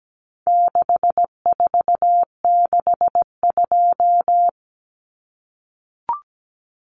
Morse code, heard as a keyed tone, about 17 wpm, 700 Hz. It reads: 6462
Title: Numbers - 4-Digit 17wpm